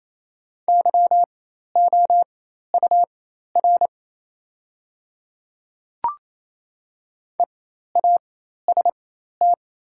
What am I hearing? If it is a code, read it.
Morse code, heard as a keyed tone, about 28 wpm, spaced as for 13 wpm, 700 Hz. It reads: YOUR EAST
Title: 4x Character Spacing - Top 500 Words - 28wpm